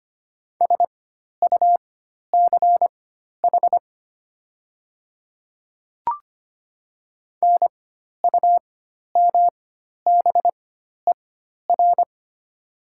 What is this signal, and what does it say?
Morse code, heard as a keyed tone, about 25 wpm, spaced as for 12 wpm, 700 Hz. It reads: SUCH NUMBER
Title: 4x Character Spacing - Top 300 Words - 25wpm